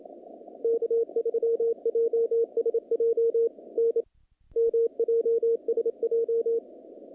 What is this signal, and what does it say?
Morse code, heard as a keyed tone, about 28 wpm, 460 Hz. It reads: K3JSJ N MJSJ
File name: contests / naqp-cw-k5tr-op-k5pi- -2100